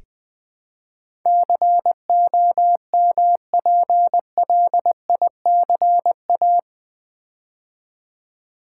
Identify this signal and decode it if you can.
Morse code, heard as a keyed tone, about 20 wpm, 700 Hz. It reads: COMPLICA
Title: Build A Word - 4 Letters At A Time - 11-Letter Words - 20wpm